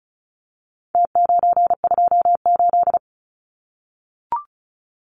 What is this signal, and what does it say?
Morse code, heard as a keyed tone, about 34 wpm, 700 Hz. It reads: T928
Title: Numbers - Contest Style - 4-Digit 35wpm